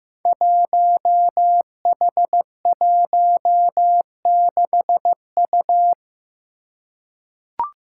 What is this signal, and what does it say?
Morse code, heard as a keyed tone, about 15 wpm, 700 Hz. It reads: 1H16U
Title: Five Challenging Letter-Number Combinations 15wpm